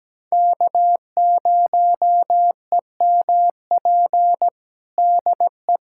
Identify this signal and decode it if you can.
Morse code, heard as a keyed tone, about 17 wpm, 700 Hz. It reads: K0EMP DE
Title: USA Callsign DE Callsign - 5 Characters - 17wpm